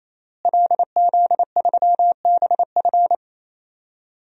Morse code, heard as a keyed tone, about 28 wpm, 700 Hz. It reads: LZ3BF
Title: DXCC-100 Call Signs - 5 Characters - Repeated 3x - 28wpm